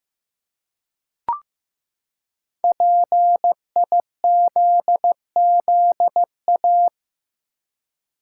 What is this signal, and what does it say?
Morse code, heard as a keyed tone, about 15 wpm, 700 Hz. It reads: PIZZA
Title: Five Letter Words - 15wpm